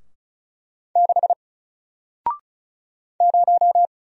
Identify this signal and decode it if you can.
Morse code, heard as a keyed tone, about 36 wpm, 700 Hz. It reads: 6 0
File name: Single-Letter-Number-35wpm